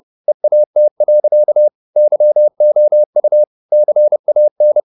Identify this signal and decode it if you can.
Morse code, heard as a keyed tone, about 30 wpm, 600 Hz. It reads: EAT. YOU CAN
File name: the old man and the sea / the old man and the sea 1 30wpm